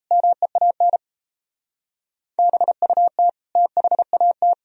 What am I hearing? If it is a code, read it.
Morse code, heard as a keyed tone, about 35 wpm, 700 Hz. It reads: MEAN BUT THAT
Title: Speed-Racing - Sets of 5 Words 28wpm